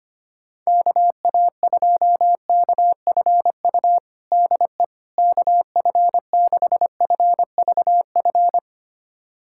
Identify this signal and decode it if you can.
Morse code, heard as a keyed tone, about 25 wpm, 700 Hz. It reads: KA2KFU DE KF6FVF
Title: USA Callsign DE Callsign - 6 Characters - 25wpm